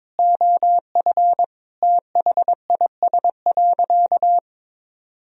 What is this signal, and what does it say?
Morse code, heard as a keyed tone, about 22 wpm, 700 Hz. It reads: OF THIS.
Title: Sentences from Top 1000 Words - Review Part 3 - 22wpm